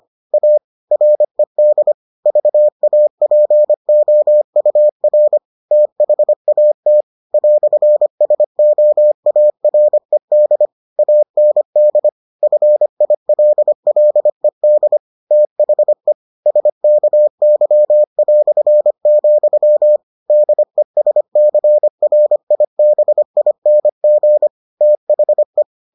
Morse code, heard as a keyed tone, about 25 wpm, 600 Hz. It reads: A RED VAPOUR THAT "SOARED AND FILLED THE SKY", DESCRIBING THE